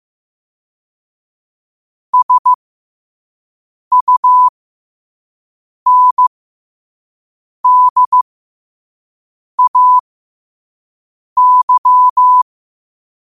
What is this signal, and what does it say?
Morse code, heard as a keyed tone, about 15 wpm, spaced as for 5 wpm, 1000 Hz. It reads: SUNDAY
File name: qotdfiles / 5wpm